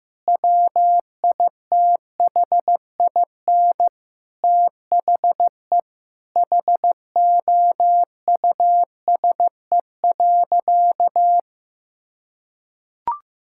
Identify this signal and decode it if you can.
Morse code, heard as a keyed tone, about 15 wpm, 700 Hz. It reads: WITHIN THE HOUSE.
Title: Sentences from Top 1000 Words - Review Part 5 - 15wpm